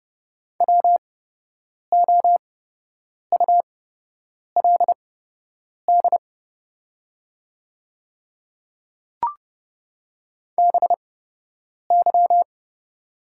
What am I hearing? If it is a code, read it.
Morse code, heard as a keyed tone, about 30 wpm, spaced as for 8 wpm, 700 Hz. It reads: WOULD BY
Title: 8x Character Spacing - Top 100 Words - 30wpm